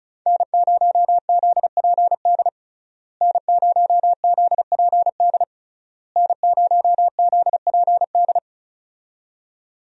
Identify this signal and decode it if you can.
Morse code, heard as a keyed tone, about 36 wpm, 700 Hz. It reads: N0ZPD N0ZPD N0ZPD
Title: USA Call Signs - 5 Characters - Repeated 3x - 35wpm